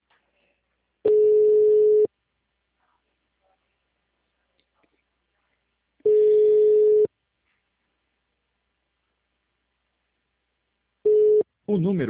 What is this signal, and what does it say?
Ringback tone (Europe, 425 Hz), with speech.